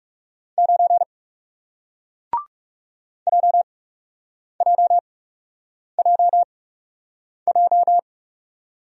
Morse code, heard as a keyed tone, about 40 wpm, spaced as for 8 wpm, 700 Hz. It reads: 9 JJJJ